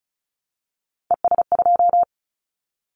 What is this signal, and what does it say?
Morse code, heard as a keyed tone, about 36 wpm, 700 Hz. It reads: ES2
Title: Callsign Prefixes for Top 50 DXCC - 35wpm